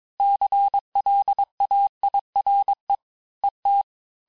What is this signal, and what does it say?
Morse code, heard as a keyed tone, about 22 wpm, 800 Hz. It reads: CLAIRE ET
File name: cours-du-mercredi / audios / cours-854-liste-fin-QSO-vitesse-21-mpm